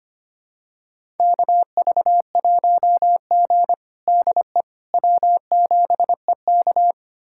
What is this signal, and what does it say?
Morse code, heard as a keyed tone, about 25 wpm, 700 Hz. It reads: KV1G DE W7EK
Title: USA Callsign DE Callsign - 4 Characters - 25wpm